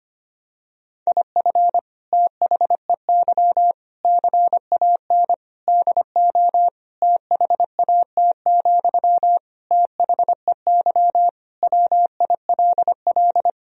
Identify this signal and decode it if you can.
Morse code, heard as a keyed tone, about 25 wpm, 700 Hz. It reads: IF THEY CAN DO THAT, THEY WILL
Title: Sentences from Top 1000 Words - Review Part 4 - 25wpm